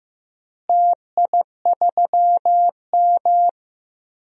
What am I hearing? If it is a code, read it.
Morse code, heard as a keyed tone, about 15 wpm, 700 Hz. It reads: TI3M
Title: DXCC-100 Call Signs - 4 Characters - Repeated 2x - 15wpm